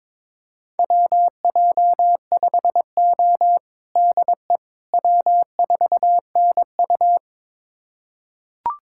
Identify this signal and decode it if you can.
Morse code, heard as a keyed tone, about 22 wpm, 700 Hz. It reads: WJ5O DE W4NU
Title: USA Callsign DE Callsign - 4 Characters - 22wpm